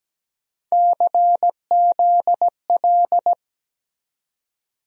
Morse code, heard as a keyed tone, about 17 wpm, 700 Hz. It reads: CZL
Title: USA Call Sign Postfixes - 3 Letters 17wpm